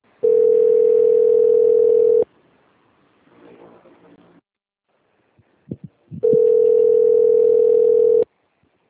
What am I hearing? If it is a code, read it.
Ringback tone (North America), 440 + 480 Hz.
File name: out- - - - - .17259